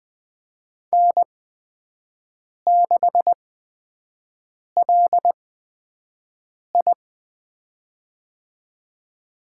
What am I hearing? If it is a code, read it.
Morse code, heard as a keyed tone, about 20 wpm, spaced as for 5 wpm, 700 Hz. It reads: N6LI
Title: USA Call Signs - 8x Character Spacing - 20wpm